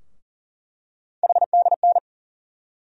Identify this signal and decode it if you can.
Morse code, heard as a keyed tone, about 40 wpm, 700 Hz. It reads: HDN
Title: USA Call Sign Postfixes - 3 Letters 40wpm